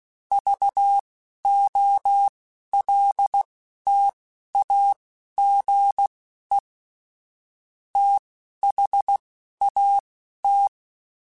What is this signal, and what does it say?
Morse code, heard as a keyed tone, about 16 wpm, spaced as for 11 wpm, 800 Hz. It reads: VOLTAGE THAT